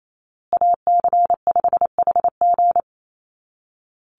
Morse code, heard as a keyed tone, about 28 wpm, 700 Hz. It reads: AC5HG
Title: USA Call Signs Repeated 2x 28wpm